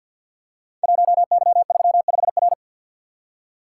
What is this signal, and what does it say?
Morse code, heard as a keyed tone, about 46 wpm, 700 Hz. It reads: 1Y35R
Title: Code Groups - 5 Letter:Numbers - 50wpm